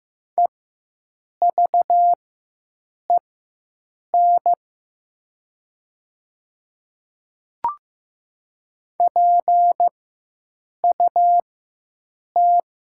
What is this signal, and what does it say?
Morse code, heard as a keyed tone, about 15 wpm, spaced as for 7 wpm, 700 Hz. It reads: EVEN PUT
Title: 4x Character Spacing - Top 200 Words - 15wpm